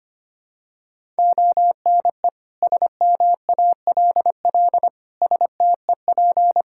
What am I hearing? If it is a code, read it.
Morse code, heard as a keyed tone, about 25 wpm, 700 Hz. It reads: ONE SMALL STEP
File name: Sets of 9 Words - 25wpm